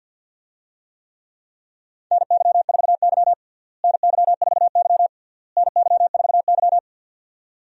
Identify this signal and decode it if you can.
Morse code, heard as a keyed tone, about 52 wpm, 700 Hz. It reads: NY4Y NY4Y NY4Y
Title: USA Call Signs - 4 Characters - Repeated 3x - 50wpm